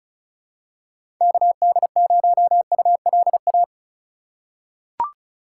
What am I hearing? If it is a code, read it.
Morse code, heard as a keyed tone, about 36 wpm, 700 Hz. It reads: KD0ULA
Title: USA Call Signs - 6 Characters - 4x Character Spacing - 35wpm